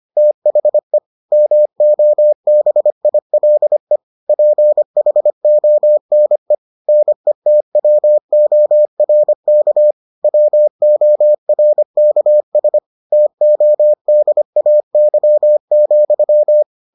Morse code, heard as a keyed tone, about 25 wpm, 600 Hz. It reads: THE MOBILE PHONE NETWORK WORKS TODAY,